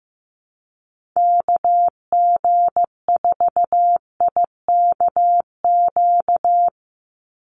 Morse code, heard as a keyed tone, about 15 wpm, 700 Hz. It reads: KG4IKQ
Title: Call Signs Repeated 2x - 15wpm